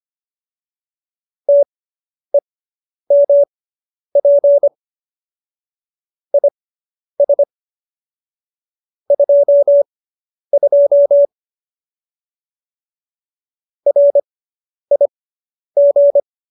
Morse code, heard as a keyed tone, about 25 wpm, spaced as for 10 wpm, 570 Hz. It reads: TEMP IS 22 RIG